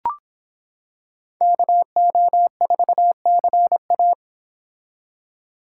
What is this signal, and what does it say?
Morse code, heard as a keyed tone, about 26 wpm, 700 Hz. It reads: KO4CA